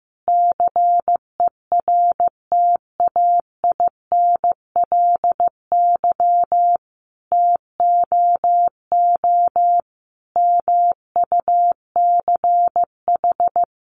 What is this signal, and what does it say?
Morse code, heard as a keyed tone, about 15 wpm, 700 Hz. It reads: CERTAINLY TOO MUCH
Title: Sentences from Top 900 Words 15wpm